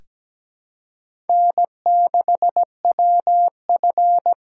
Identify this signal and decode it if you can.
Morse code, heard as a keyed tone, about 17 wpm, 700 Hz. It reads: N6WF
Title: USA Callsign DE Callsign - 4 Characters - 17wpm